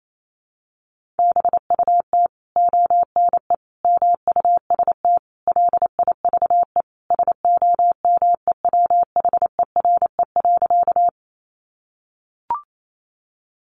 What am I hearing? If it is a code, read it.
Morse code, heard as a keyed tone, about 28 wpm, 700 Hz. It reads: BUT ONE MUST LIVE SOMEWHERE.
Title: Sentences from Top 1400 Words 28wpm